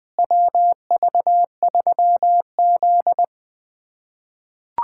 Morse code, heard as a keyed tone, about 20 wpm, 700 Hz. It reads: WV3Z
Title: USA Callsign DE Callsign - 4 Characters - 20wpm